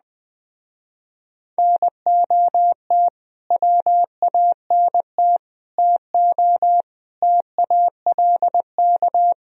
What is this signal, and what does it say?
Morse code, heard as a keyed tone, about 20 wpm, 700 Hz. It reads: NOT WANT TO TALK